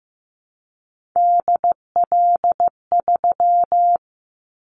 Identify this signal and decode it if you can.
Morse code, heard as a keyed tone, about 15 wpm, 700 Hz. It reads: DL3
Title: Callsign Prefixes for Top 25 DXCC - 15wpm